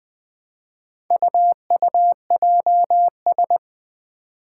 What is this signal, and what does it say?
Morse code, heard as a keyed tone, about 20 wpm, 700 Hz. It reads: UUJS
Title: Four Challenging Letter-Number Combinations 20wpm